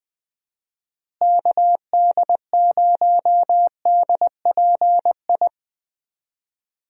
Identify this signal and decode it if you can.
Morse code, heard as a keyed tone, about 20 wpm, 700 Hz. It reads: KD0DPI
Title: USA Call Signs - 6 Characters - Repeated 2x - 20wpm